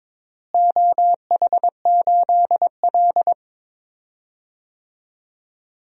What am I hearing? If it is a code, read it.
Morse code, heard as a keyed tone, about 22 wpm, 700 Hz. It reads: OH8L
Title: DXCC-100 Call Signs - 4 Characters - Repeated 2x - 22wpm